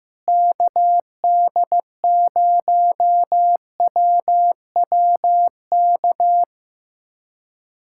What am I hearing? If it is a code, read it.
Morse code, heard as a keyed tone, about 16 wpm, 700 Hz. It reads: KD0WWK
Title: USA Callsign DE Callsign 15wpm